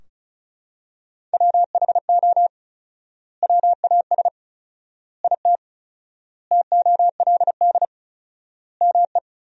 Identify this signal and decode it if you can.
Morse code, heard as a keyed tone, about 36 wpm, 700 Hz. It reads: WHO WAS IT TOLD ME
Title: 4x Word-Spacing - Sentences from Top 400 Words - 35wpm